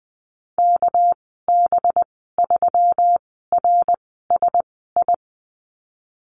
Morse code, heard as a keyed tone, about 20 wpm, spaced as for 15 wpm, 700 Hz. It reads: KB3RSI
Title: 207 Lesson 33g - Letter j - Call Signs 20wpm